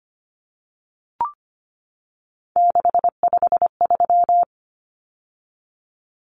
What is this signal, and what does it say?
Morse code, heard as a keyed tone, about 25 wpm, 700 Hz. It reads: 653
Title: Numbers - 3-Digit 25wpm